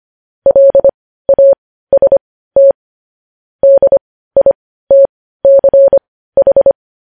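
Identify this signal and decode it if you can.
Morse code, heard as a keyed tone, about 25 wpm, spaced as for 15 wpm, 550 Hz. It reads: LAST DITCH
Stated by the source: Morse code news headlines practice at 15, preceded by vvv.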